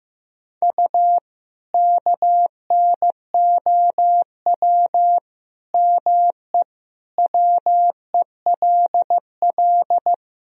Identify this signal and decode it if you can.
Morse code, heard as a keyed tone, about 15 wpm, 700 Hz. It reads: U KNOW ME WELL
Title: Sentences from Top 400 Words and Abbreviations 15wpm